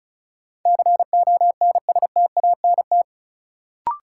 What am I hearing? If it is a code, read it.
Morse code, heard as a keyed tone, about 36 wpm, 700 Hz. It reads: CONSTANT
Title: Sentences from Top 1500 Words - New Word Drill Alt - 35wpm